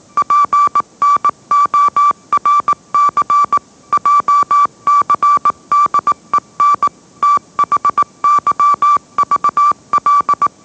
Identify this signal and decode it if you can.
Morse code, heard as a keyed tone, about 18 wpm, 1160 Hz. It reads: PNORC JCDEN THYVL
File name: mrx / mrx082 20wpm